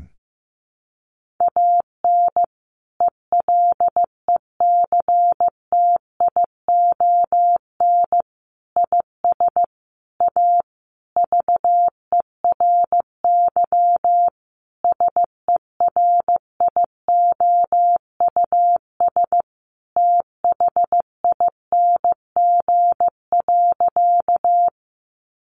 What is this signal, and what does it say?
Morse code, heard as a keyed tone, about 15 wpm, 700 Hz. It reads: AN ELECTION IS A VERY SERIOUS THING.
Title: Sentences from Top 1500 Words 15wpm